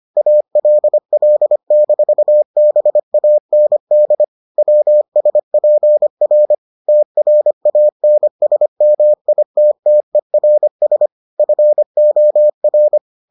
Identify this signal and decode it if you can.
Morse code, heard as a keyed tone, about 25 wpm, 600 Hz. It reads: ALL-BAND WSPR TRANSMITTERS FOR